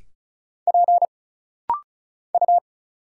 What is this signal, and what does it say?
Morse code, heard as a keyed tone, about 35 wpm, 700 Hz. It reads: P U
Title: Single Letters - Rapid-Fire 35wpm